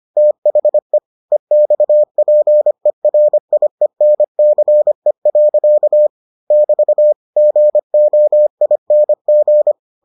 Morse code, heard as a keyed tone, about 25 wpm, 600 Hz. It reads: THE EXPERIENCE. <BT> GOING